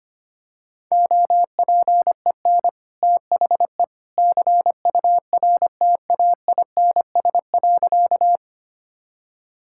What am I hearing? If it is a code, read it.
Morse code, heard as a keyed tone, about 25 wpm, 700 Hz. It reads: OPEN THE CURTAINS.
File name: Easy Sentences with Hints - 25wpm